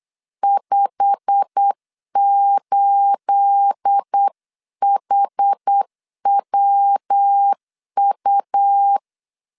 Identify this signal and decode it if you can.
Morse code, heard as a keyed tone, about 8 wpm, 800 Hz. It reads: 58HWU
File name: CW / DET / 1 9 1500C S B 8wpm 5624